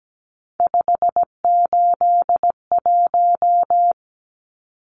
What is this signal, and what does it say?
Morse code, heard as a keyed tone, about 17 wpm, 700 Hz. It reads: 581
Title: Numbers - 3-Digit 17wpm